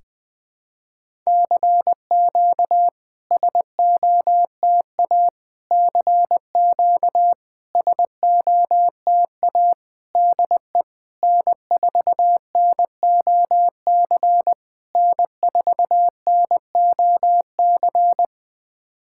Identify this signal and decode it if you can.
Morse code, heard as a keyed tone, about 21 wpm, 700 Hz. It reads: CQ SOTA CQ SOTA DE N4NOC N4NOC
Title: SOTA QSO Practice 20wpm